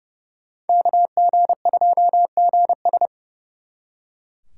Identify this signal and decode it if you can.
Morse code, heard as a keyed tone, about 37 wpm, 700 Hz. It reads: KG2GS
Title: Speed Racing - USA 2x2 Call Signs 30wpm